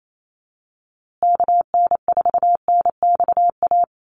Morse code, heard as a keyed tone, about 28 wpm, 700 Hz. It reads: KN4NXA
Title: USA Call Signs Repeated 3x 28wpm